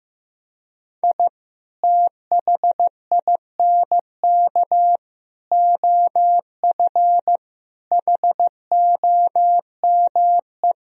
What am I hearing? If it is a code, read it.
Morse code, heard as a keyed tone, about 15 wpm, 700 Hz. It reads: I THINK OF HOME
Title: Sentences from Top 1000 Words - Review Part 3 - 15wpm